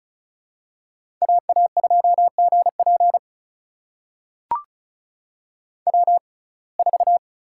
Morse code, heard as a keyed tone, about 36 wpm, spaced as for 25 wpm, 700 Hz. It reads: AA2GP W4
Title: USA Call Signs - 5 Characters - 6x Character Spacing - 35wpm